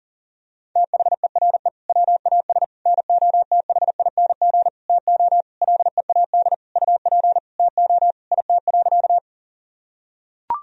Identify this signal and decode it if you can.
Morse code, heard as a keyed tone, about 40 wpm, 700 Hz. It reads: THERE WAS NOTHING TO LEAD UP TO IT.
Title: Sentences from Top 1000 Words - Review Part 2 - 40wpm